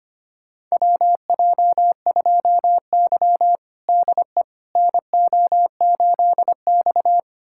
Morse code, heard as a keyed tone, about 25 wpm, 700 Hz. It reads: WJ2Y DE NO8X
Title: USA Callsign DE Callsign - 4 Characters - 25wpm